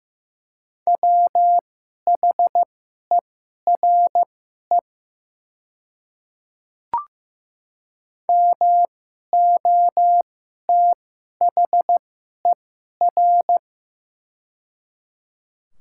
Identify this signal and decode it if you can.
Morse code, heard as a keyed tone, about 15 wpm, spaced as for 11 wpm, 700 Hz. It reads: WHERE MOTHER
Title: 2x Character Spacing - Top 200 Words - 15wpm